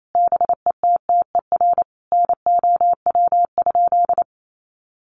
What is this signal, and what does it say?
Morse code, heard as a keyed tone, about 28 wpm, 700 Hz. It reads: BETTER NOW?
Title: Sentences from Top 1200 Words 28wpm